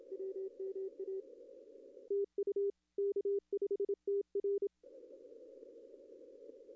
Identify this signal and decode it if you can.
Morse code, heard as a keyed tone, about 28 wpm, 410 Hz. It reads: WMA TU K5TR
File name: contests / arrl-160-k5tr-op-w5tn -0800